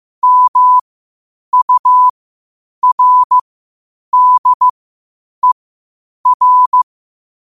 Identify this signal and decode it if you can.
Morse code, heard as a keyed tone, about 15 wpm, spaced as for 8 wpm, 1000 Hz. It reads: MURDER
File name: qotdfiles / 8wpm